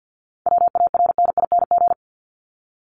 Morse code, heard as a keyed tone, about 50 wpm, 700 Hz. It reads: WARNING
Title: Top 1751-2000 Words 50wpm